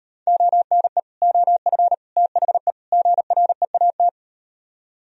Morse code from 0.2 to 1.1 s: ONE

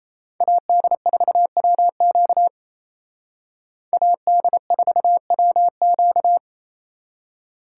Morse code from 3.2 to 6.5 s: AD4WQ

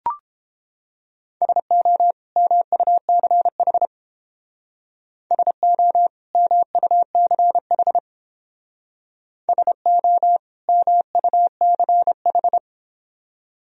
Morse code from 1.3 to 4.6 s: SO MUCH

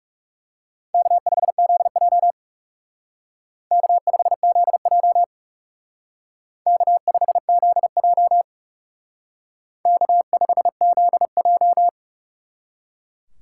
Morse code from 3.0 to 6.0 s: K5ZJ

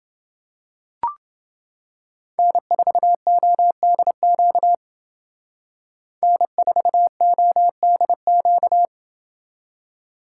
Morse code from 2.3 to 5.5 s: N4ODQ

